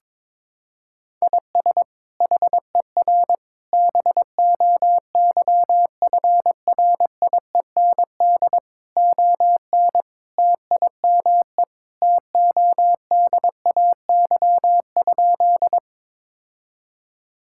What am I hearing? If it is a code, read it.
Morse code, heard as a keyed tone, about 22 wpm, 700 Hz. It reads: IS HER BOYFRIEND ON TIME TODAY?
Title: Sentences from Top 1000 Words - Review Part 5 - 22wpm